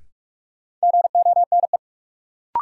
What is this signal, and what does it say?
Morse code, heard as a keyed tone, about 46 wpm, 700 Hz. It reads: GONE
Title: Sentences from Top 1200 Words - New Word Drill - 45wpm